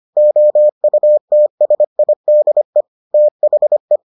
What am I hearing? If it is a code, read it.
Morse code, heard as a keyed tone, about 25 wpm, 600 Hz. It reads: OUTSIDE THE